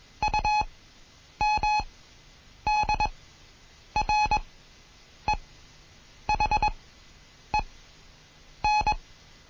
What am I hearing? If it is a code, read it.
Morse code, heard as a keyed tone, about 21 wpm, spaced as for 8 wpm, 850 Hz. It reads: UMDREHEN
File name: Morsen / Kurs A 20